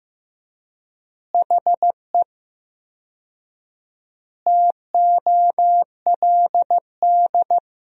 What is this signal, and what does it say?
Morse code, heard as a keyed tone, about 15 wpm, 700 Hz. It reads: HE TOLD
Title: 4x Word-Spacing - Sentences from Top 500 Words - 15wpm